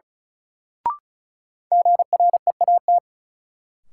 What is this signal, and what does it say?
Morse code, heard as a keyed tone, about 36 wpm, 700 Hz. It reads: GREAT
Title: Sentences from Top 200 Words - New Word Drill - 35wpm